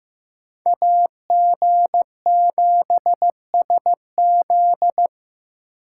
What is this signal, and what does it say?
Morse code, heard as a keyed tone, about 15 wpm, 700 Hz. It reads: AG7SZ
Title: USA Call Signs - 5 Characters - 2x Character Spacing - 15wpm